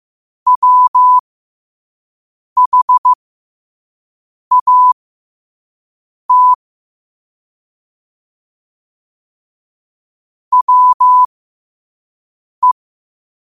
Morse code, heard as a keyed tone, about 15 wpm, spaced as for 5 wpm, 1000 Hz. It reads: WHAT WE
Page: New quotes every day in morse code at 5 Words per minute.